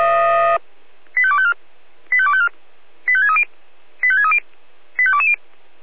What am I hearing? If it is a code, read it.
Five-tone selective call (ZVEI), heard as quick runs of five tones, tones of about 70 ms. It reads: 85254 85254 85630 85630 85220